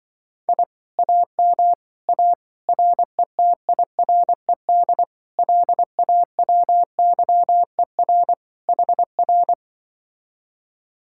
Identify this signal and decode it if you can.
Morse code, heard as a keyed tone, about 24 wpm, 700 Hz. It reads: I AM A RETIRED LAWYER HR
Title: Speed-Racing Common QSO Phrases - 20wpm